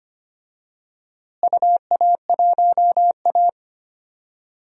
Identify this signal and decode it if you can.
Morse code, heard as a keyed tone, about 25 wpm, 700 Hz. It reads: UA1A
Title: DXCC-100 Call Signs - 4 Characters - Repeated 2x - 25wpm